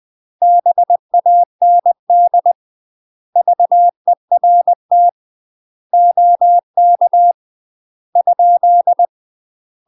Morse code, heard as a keyed tone, about 20 wpm, 700 Hz. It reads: BAND VERT OK ?
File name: UK MASTER QSOs TRAINING / 20WPM / UK MASTER QSOs 0073